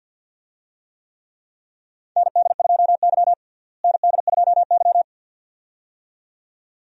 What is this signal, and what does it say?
Morse code, heard as a keyed tone, about 46 wpm, 700 Hz. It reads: ND2Y ND2Y
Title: USA Call Signs - 4 Characters - Repeated 2x - 50wpm